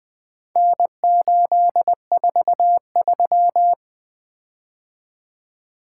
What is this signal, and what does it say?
Morse code, heard as a keyed tone, about 20 wpm, 700 Hz. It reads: N843
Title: Four Letter-Number Combinations - 20wpm